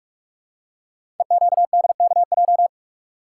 Morse code, heard as a keyed tone, about 44 wpm, 700 Hz. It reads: EQDKJ
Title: Code Groups - 5 Letters - 45wpm